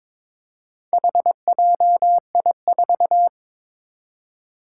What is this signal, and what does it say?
Morse code, heard as a keyed tone, about 23 wpm, 700 Hz. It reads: HJI4
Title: Four Challenging Letter-Number Combinations 22wpm